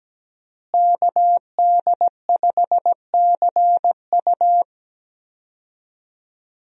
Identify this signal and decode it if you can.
Morse code, heard as a keyed tone, about 17 wpm, 700 Hz. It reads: KD5CU
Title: USA Call Signs - 5 Characters - Repeated 3x - 17wpm